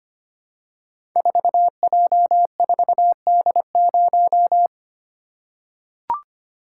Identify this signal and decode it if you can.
Morse code, heard as a keyed tone, about 25 wpm, 700 Hz. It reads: 4J4D0
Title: Code Groups - 5 Letter:Numbers - 25wpm